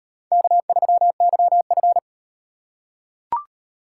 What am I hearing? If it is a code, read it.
Morse code, heard as a keyed tone, about 34 wpm, 700 Hz. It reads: K3YF